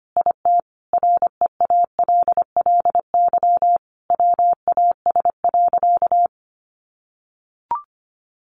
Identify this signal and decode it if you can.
Morse code, heard as a keyed tone, about 25 wpm, 700 Hz. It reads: IT REALLY WAS.
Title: Sentences from Top 600 Words 25wpm